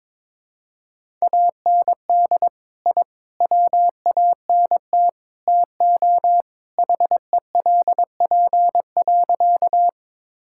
Morse code, heard as a keyed tone, about 22 wpm, 700 Hz. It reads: AND I WANT TO HELP.
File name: practice / 22wpm / Sentences with 4-Word Phrases - 22wpm